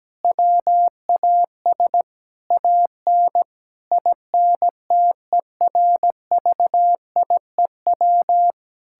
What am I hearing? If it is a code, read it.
Morse code, heard as a keyed tone, about 17 wpm, 700 Hz. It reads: WAS AN INTERVIEW
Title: Sentences from Top 1600 Words 17wpm